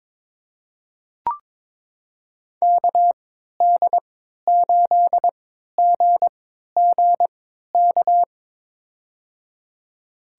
Morse code, heard as a keyed tone, about 22 wpm, spaced as for 13 wpm, 700 Hz. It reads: KD8GGK
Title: USA Call Signs - 6 Characters - 3x Character Spacing - 22wpm